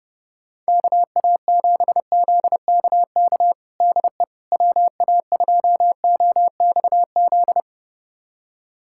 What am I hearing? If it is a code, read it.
Morse code, heard as a keyed tone, about 32 wpm, 700 Hz. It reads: KA7ZKK DE WA2OXZ
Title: USA Callsign DE Callsign - 6 Characters - 30wpm